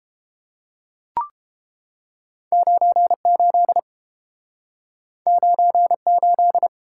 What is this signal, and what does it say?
Morse code, heard as a keyed tone, about 30 wpm, 700 Hz. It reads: 98 98